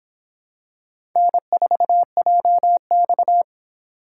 Morse code, heard as a keyed tone, about 26 wpm, 700 Hz. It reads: N4JX